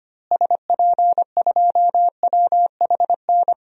Morse code, heard as a keyed tone, about 25 wpm, 700 Hz. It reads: SP2WHN
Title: DXCC-100 Call Signs - 6 Characters - Repeated 2x - 25wpm